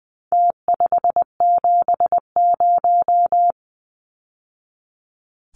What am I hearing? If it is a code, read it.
Morse code, heard as a keyed tone, about 20 wpm, 700 Hz. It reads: T570
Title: Numbers - Contest Style - 4-Digit 20wpm